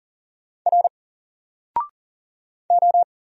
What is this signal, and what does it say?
Morse code, heard as a keyed tone, about 40 wpm, 700 Hz. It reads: R O
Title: Single Character (Letter:Number:Punctuation:Prosign) 40wpm